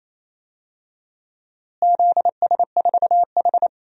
Morse code, heard as a keyed tone, about 28 wpm, 700 Hz. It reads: ZS4H
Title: DXCC-100 Call Signs - 4 Characters - Repeated 3x - 28wpm